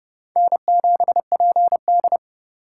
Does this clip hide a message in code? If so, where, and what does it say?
Yes: Morse code from 0.4 to 2.2 s, heard as a keyed tone, about 31 wpm, 700 Hz. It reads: N7PD